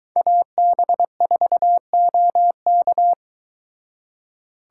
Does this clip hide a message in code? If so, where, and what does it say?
Yes: Morse code from 0.2 to 3.1 s, heard as a keyed tone, about 23 wpm, 700 Hz. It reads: AB4OK